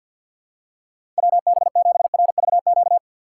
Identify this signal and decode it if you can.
Morse code, heard as a keyed tone, about 52 wpm, 700 Hz. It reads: WB7RVQ
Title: USA Call Signs - 6 Characters - 3x Character Spacing - 50wpm